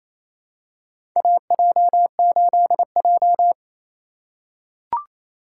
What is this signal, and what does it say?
Morse code, heard as a keyed tone, about 28 wpm, 700 Hz. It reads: AJ8J
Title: USA Call Signs - 4 Characters - 2x Character Spacing - 28wpm